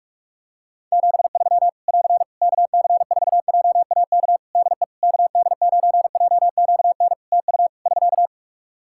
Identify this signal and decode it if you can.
Morse code, heard as a keyed tone, about 46 wpm, 700 Hz. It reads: 73 <AR> KC4JAK DE KD9JQN TU <SK>
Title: Practice QSO - Exchange 4 - Ending - 45wpm